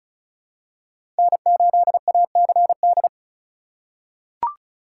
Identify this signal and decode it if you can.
Morse code, heard as a keyed tone, about 36 wpm, 700 Hz. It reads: N8ACD
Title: USA Call Signs - 5 Characters - 4x Character Spacing - 35wpm